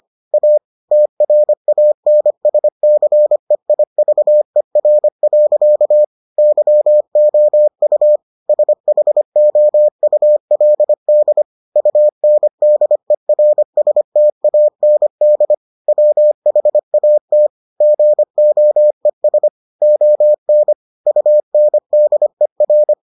Morse code, heard as a keyed tone, about 25 wpm, 600 Hz. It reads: A TRANSCEIVER. YOU SHOULD UNDERSTAND WHAT GOES ON UNDER